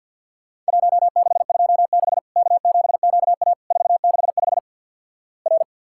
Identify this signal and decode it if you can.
Morse code, heard as a keyed tone, about 50 wpm, 700 Hz. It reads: 1626 K7QA 465 R
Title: Worked All Europe DX Contest - QTC Exchange with Repeats - 50wpm